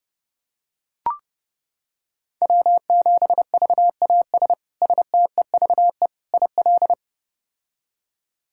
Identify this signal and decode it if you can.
Morse code, heard as a keyed tone, about 32 wpm, 700 Hz. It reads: W7VAS STEVE IL
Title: K1USN SST QSO Practice - 30wpm